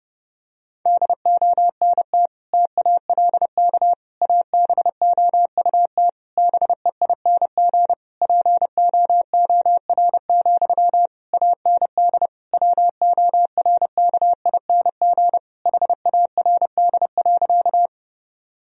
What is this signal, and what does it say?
Morse code, heard as a keyed tone, about 32 wpm, 700 Hz. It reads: DONT TALK ABOUT BEING POOR, AND WORKING HARD.
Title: Sentences from Top 1000 Words - Review Part 4 - 30wpm